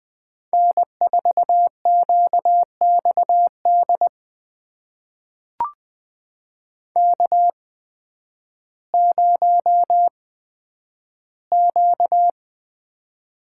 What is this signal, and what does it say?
Morse code, heard as a keyed tone, about 20 wpm, spaced as for 11 wpm, 700 Hz. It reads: N4QXD K0Q
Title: USA Call Signs - 8x Character Spacing - 20wpm